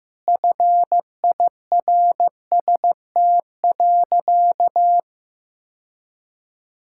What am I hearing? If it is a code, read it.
Morse code, heard as a keyed tone, about 15 wpm, 700 Hz. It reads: FIRST.
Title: Sentences from Top 1000 Words - Review Part 6 - 15wpm